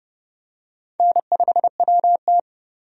Morse code, heard as a keyed tone, about 30 wpm, 700 Hz. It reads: N5WT
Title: USA Call Signs - 4 Characters - 30wpm